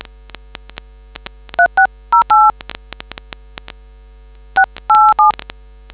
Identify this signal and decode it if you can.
DTMF, heard as telephone touch tones, tones of about 110 ms. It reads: 3608 687